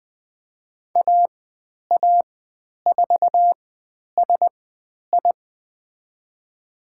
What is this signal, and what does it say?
Morse code, heard as a keyed tone, about 20 wpm, spaced as for 10 wpm, 700 Hz. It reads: AA4SI
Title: 109 Lesson 18g - Letter m - Call Signs 20wpm